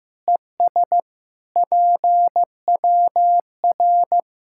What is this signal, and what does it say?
Morse code, heard as a keyed tone, about 15 wpm, 700 Hz. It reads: ES PWR
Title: Practice QSO - Exchange 2 - Info - 15wpm